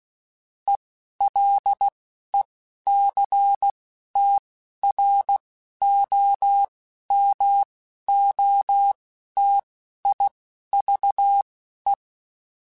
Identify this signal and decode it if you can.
Morse code, heard as a keyed tone, about 16 wpm, spaced as for 11 wpm, 800 Hz. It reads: ELECTROMOTIVE
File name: Technical Subjects / Technical Subjects-mp3s / XL @10